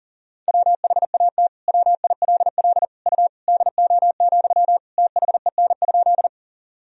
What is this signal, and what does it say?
Morse code, heard as a keyed tone, about 40 wpm, 700 Hz. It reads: WHAT WILL U DO, THEN?
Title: Sentences from Top 100 Words and Abbreviations 40wpm